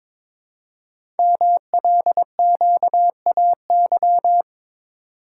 Morse code, heard as a keyed tone, about 23 wpm, 700 Hz. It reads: MLQAY
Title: Code Groups - 5 Letters - 22wpm